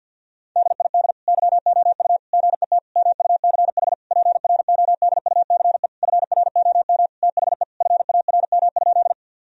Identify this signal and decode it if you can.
Morse code, heard as a keyed tone, about 50 wpm, 700 Hz. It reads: DID YOU GET MUCH PRODUCE FROM THE FARM?
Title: Sentences from Top 1000 Words - Review Part 3 - 50wpm